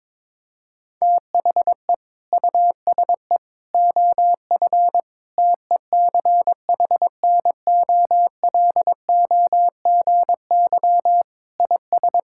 Morse code, heard as a keyed tone, about 22 wpm, 700 Hz. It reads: THE USE OF TECHNOLOGY IS
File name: practice / 22wpm / Sentences with 2-Word Phrases - 22wpm